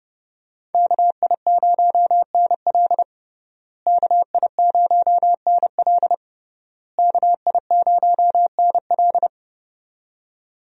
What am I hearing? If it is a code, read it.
Morse code, heard as a keyed tone, about 29 wpm, 700 Hz. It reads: KI0NL KI0NL KI0NL
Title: USA Call Signs - 5 Characters - Repeated 3x - 30wpm